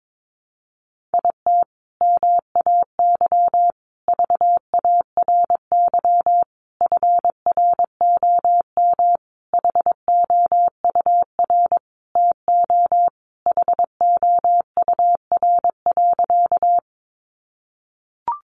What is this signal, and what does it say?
Morse code, heard as a keyed tone, about 22 wpm, 700 Hz. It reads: IT MAY VARY FROM HOUR TO HOUR.
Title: Sentences from Top 1200 Words 22wpm